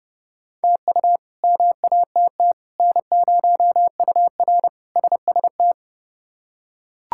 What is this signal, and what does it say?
Morse code, heard as a keyed tone, about 29 wpm, 700 Hz. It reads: TU MATT N0UR SST
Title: K1USN SST QSO Practice - 30wpm